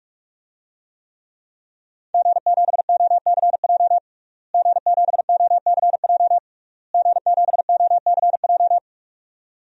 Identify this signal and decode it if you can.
Morse code, heard as a keyed tone, about 47 wpm, 700 Hz. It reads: G7OCJ G7OCJ G7OCJ
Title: DXCC-100 Call Signs - 5 Characters - Repeated 3x - 45wpm